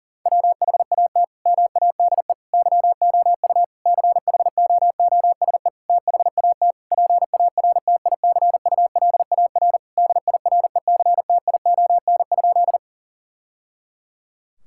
Morse code, heard as a keyed tone, about 37 wpm, 700 Hz. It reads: WHAT MADE YOU CHOOSE THAT PARTICULAR DIRECTION?
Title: Sentences from Top 1000 Words - Review Part 5 - 40wpm